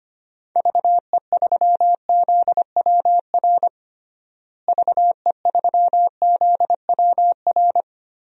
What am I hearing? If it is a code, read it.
Morse code, heard as a keyed tone, about 25 wpm, 700 Hz. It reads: VE3ZWR VE3ZWR
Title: DXCC-100 Call Signs - 6 Characters - Repeated 3x - 25wpm